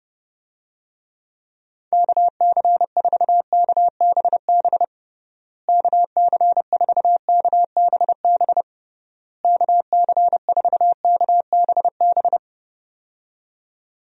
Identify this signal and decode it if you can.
Morse code, heard as a keyed tone, about 30 wpm, 700 Hz. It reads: KC4KBB KC4KBB KC4KBB
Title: USA Call Signs - 6 Characters - Repeated 3x - 30wpm